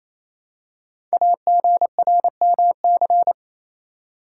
Morse code, heard as a keyed tone, about 27 wpm, 700 Hz. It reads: AGRMC
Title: Code Groups - 5 Letters - 28wpm